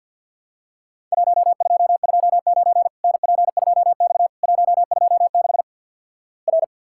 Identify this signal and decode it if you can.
Morse code, heard as a keyed tone, about 50 wpm, 700 Hz. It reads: 1229 NP2X 126 R
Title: Worked All Europe DX Contest - QTC Exchange - 50wpm